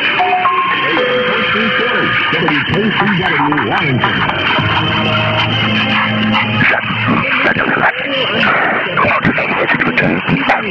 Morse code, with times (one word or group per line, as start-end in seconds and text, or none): none